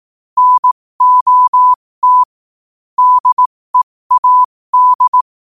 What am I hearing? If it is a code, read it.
Morse code, heard as a keyed tone, about 18 wpm, spaced as for 15 wpm, 1000 Hz. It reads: NOT DEAD
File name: qotdfiles / 15wpm